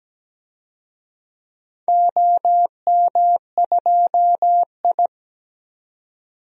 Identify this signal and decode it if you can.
Morse code, heard as a keyed tone, about 17 wpm, 700 Hz. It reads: OM2I
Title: DXCC-100 Call Signs - 4 Characters - 17wpm